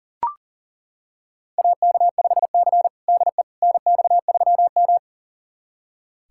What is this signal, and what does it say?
Morse code, heard as a keyed tone, about 40 wpm, 700 Hz. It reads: AK5C DE NX3M
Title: USA Callsign DE Callsign - 4 Characters - 40wpm